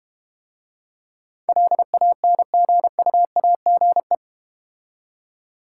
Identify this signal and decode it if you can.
Morse code, heard as a keyed tone, about 30 wpm, 700 Hz. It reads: LANGUAGE
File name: Easy Words - Speed-Racing with Hints - 25wpm